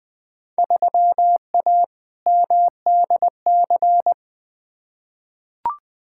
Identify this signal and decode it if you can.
Morse code, heard as a keyed tone, about 20 wpm, 700 Hz. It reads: 3A MDC
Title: Field Day Exchanges - 20wpm